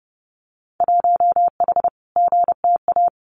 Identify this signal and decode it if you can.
Morse code, heard as a keyed tone, about 30 wpm, 700 Hz. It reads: 1H GTA
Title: Winter Field Day Exchanges 30wpm